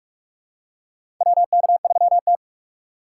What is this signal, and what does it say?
Morse code, heard as a keyed tone, about 46 wpm, 700 Hz. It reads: WK3T
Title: USA Call Signs - 4 Characters - 4x Character Spacing - 45wpm